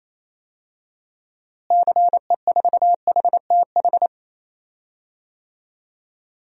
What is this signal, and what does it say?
Morse code, heard as a keyed tone, about 28 wpm, 700 Hz. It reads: CE4HTH
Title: DXCC-100 Call Signs - 6 Characters - Repeated 2x - 28wpm